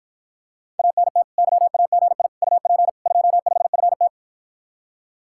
Morse code, heard as a keyed tone, about 54 wpm, 700 Hz. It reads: ANT YAGI UP 25FT